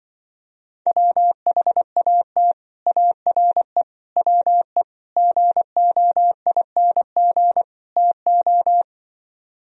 Morse code, heard as a keyed tone, about 25 wpm, 700 Hz. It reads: WHAT ARE WE GOING TO
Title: Speed-Racing - Sets of 5 Words 20wpm